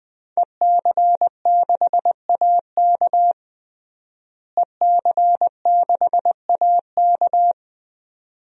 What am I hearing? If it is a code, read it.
Morse code, heard as a keyed tone, about 20 wpm, 700 Hz. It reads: EC6AK EC6AK
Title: DXCC-100 Call Signs - 5 Characters - Repeated 2x - 20wpm